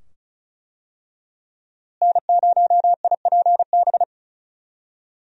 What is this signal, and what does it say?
Morse code, heard as a keyed tone, about 34 wpm, 700 Hz. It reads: N0IPB
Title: USA Call Signs - 5 Characters - Repeated 2x - 35wpm